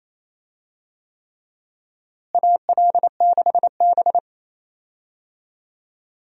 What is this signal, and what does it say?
Morse code, heard as a keyed tone, about 28 wpm, 700 Hz. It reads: AL6B
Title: USA Call Signs - 4 Characters - Repeated 3x - 28wpm